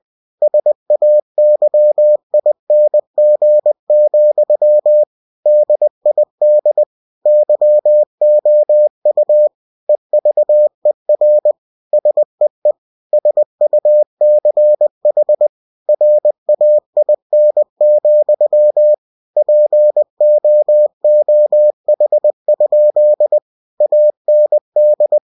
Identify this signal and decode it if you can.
Morse code, heard as a keyed tone, about 20 wpm, 600 Hz. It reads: SAYING, DID YOU EVER SEE SUCH RAIN, POOH? AND
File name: cw-books / winnie-the-pooh / winnie-20wpm-page-0091